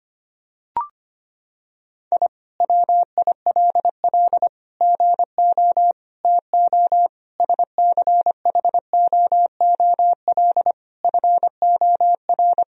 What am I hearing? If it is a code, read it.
Morse code, heard as a keyed tone, about 25 wpm, 700 Hz. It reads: I WILL GO TO SCHOOL FOR
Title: Sentences from Top 1000 Words - Review Part 4 - 25wpm